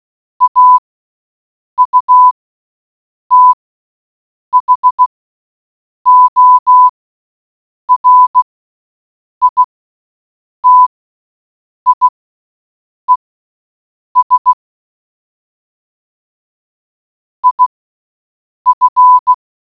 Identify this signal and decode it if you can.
Morse code, heard as a keyed tone, about 16 wpm, spaced as for 7 wpm, 1000 Hz. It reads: AUTHORITIES IF